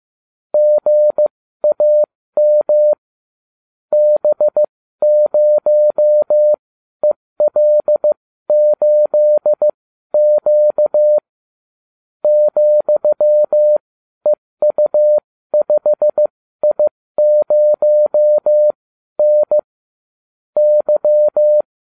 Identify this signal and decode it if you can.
Morse code, heard as a keyed tone, about 15 wpm, spaced as for 12 wpm, 600 Hz. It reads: GAM B0EL8Q ,EU5I0N Y